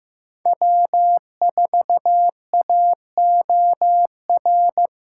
Morse code, heard as a keyed tone, about 15 wpm, 700 Hz. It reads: W4AOR
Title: USA Callsign DE Callsign - 5 Characters - 15wpm